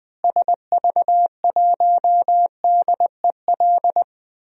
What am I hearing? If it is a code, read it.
Morse code, heard as a keyed tone, about 20 wpm, 700 Hz. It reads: SV1DEL
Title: DXCC-100 Call Signs - 6 Characters - Repeated 2x - 20wpm